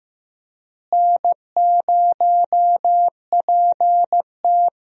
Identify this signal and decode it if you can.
Morse code, heard as a keyed tone, about 15 wpm, 700 Hz. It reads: N0PT
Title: USA Call Signs - 4 Characters - Repeated 3x - 15wpm